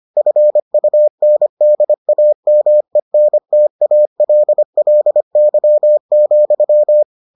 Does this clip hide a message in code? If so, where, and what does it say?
Yes: Morse code from 0.2 to 7.0 s, heard as a keyed tone, about 25 wpm, 600 Hz. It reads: FUNDAMENTALLY,